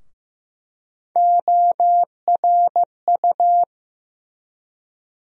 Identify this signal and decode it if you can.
Morse code, heard as a keyed tone, about 15 wpm, 700 Hz. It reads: ORU
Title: USA Call Sign Postfixes - 3 Letters 15wpm